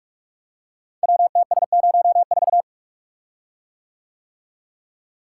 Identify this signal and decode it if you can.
Morse code, heard as a keyed tone, about 44 wpm, 700 Hz. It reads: WTS04
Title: Code Groups - 5 Letter:Numbers - 45wpm